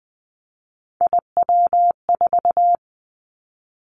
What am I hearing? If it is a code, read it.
Morse code, heard as a keyed tone, about 20 wpm, 700 Hz. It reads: IW4
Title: Callsign Prefixes for Top 50 DXCC - 20wpm